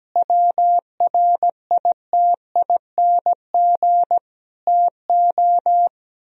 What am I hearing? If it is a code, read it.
Morse code, heard as a keyed tone, about 17 wpm, 700 Hz. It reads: WRITING TO
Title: Sentences from Top 1000 Words - Review Part 4 - 17wpm